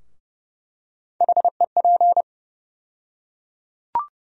Morse code, heard as a keyed tone, about 30 wpm, 700 Hz. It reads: HEP
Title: USA Call Sign Postfixes - 3 Letters 30wpm